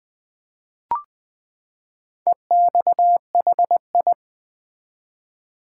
Morse code, heard as a keyed tone, about 20 wpm, 700 Hz. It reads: EXHI
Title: Build A Word - 4 Letters At A Time - 10-Letter Words - 20wpm